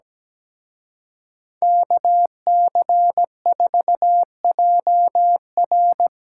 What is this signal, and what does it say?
Morse code, heard as a keyed tone, about 17 wpm, 700 Hz. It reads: KC4JR
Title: Speed-Racing Common QSO Phrases - 17wpm